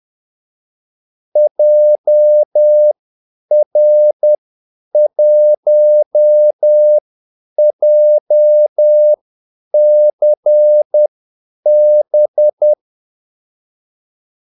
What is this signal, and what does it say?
Morse code, heard as a keyed tone, about 10 wpm, spaced as for 8 wpm, 600 Hz. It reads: JR1JCB
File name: Callsigns-10-8wpm